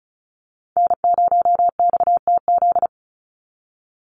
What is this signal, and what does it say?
Morse code, heard as a keyed tone, about 34 wpm, 700 Hz. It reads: N0XTZ
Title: USA Call Signs 35wpm